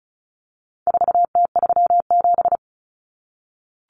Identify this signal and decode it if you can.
Morse code, heard as a keyed tone, about 34 wpm, 700 Hz. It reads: 4T37
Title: Common Cut Numbers - 4-Digit 35wpm